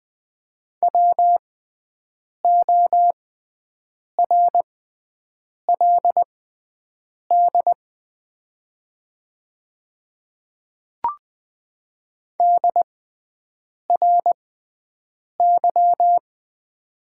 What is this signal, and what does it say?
Morse code, heard as a keyed tone, about 20 wpm, spaced as for 7 wpm, 700 Hz. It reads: WORLD DRY
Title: 6x Character Spacing - Top 500 Words - 20wpm